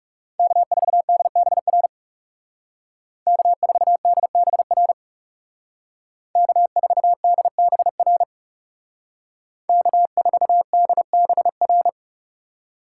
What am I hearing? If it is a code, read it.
Morse code, heard as a keyed tone, about 38 wpm, 700 Hz. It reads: K4DBR K4DBR K4DBR K4DBR